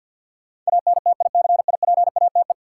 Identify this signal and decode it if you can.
Morse code, heard as a keyed tone, about 52 wpm, 700 Hz. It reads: ANTICIPATE
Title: Sentences from Top 1500 Words - New Word Drill - 50wpm